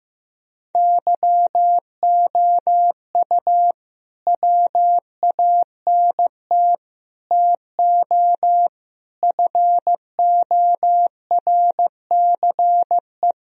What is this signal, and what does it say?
Morse code, heard as a keyed tone, about 15 wpm, 700 Hz. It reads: YOU WANT TO FORCE
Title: Sentences from Top 1000 Words - Review Part 3 - 15wpm